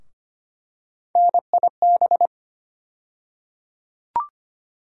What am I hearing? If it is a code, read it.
Morse code, heard as a keyed tone, about 25 wpm, 700 Hz. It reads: NIB
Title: USA Call Sign Postfixes - 3 Letters 25wpm